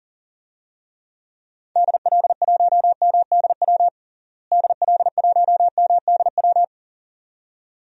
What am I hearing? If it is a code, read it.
Morse code, heard as a keyed tone, about 40 wpm, 700 Hz. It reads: DL1MDW DL1MDW
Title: DXCC-100 Call Signs - 6 Characters - Repeated 2x - 40wpm